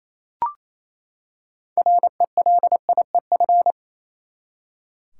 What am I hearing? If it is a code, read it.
Morse code, heard as a keyed tone, about 28 wpm, 700 Hz. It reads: RELIEF
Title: Six Letter Words - 28wpm